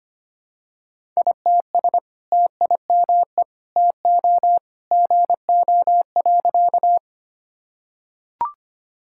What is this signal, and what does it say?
Morse code, heard as a keyed tone, about 25 wpm, 700 Hz. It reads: ITS TIME TO GO.
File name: practice / 25wpm / Sentences with 2-Word Phrases - 25wpm